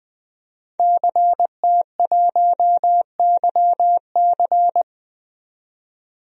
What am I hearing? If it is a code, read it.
Morse code, heard as a keyed tone, about 20 wpm, 700 Hz. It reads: CT1YC
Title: DXCC-100 Call Signs - 5 Characters - 20wpm